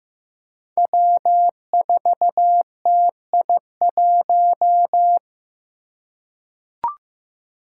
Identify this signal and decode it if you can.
Morse code, heard as a keyed tone, about 15 wpm, 700 Hz. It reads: W4TI1
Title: Five Challenging Letter-Number Combinations 15wpm